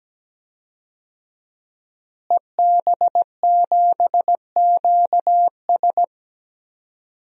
Morse code, heard as a keyed tone, about 17 wpm, 700 Hz. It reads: EB7QS
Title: DXCC-100 Call Signs - 5 Characters - Repeated 3x - 17wpm